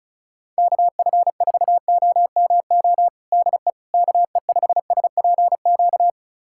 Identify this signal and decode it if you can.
Morse code, heard as a keyed tone, about 36 wpm, 700 Hz. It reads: KF4OMO DE KE5SPQ
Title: USA Callsign DE Callsign - 6 Characters - 35wpm